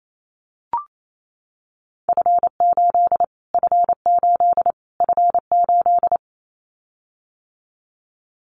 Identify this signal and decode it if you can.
Morse code, heard as a keyed tone, about 28 wpm, 700 Hz. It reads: F8 F8 F8
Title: Callsign Prefixes for Top 25 DXCC - Repeated 3x - 28wpm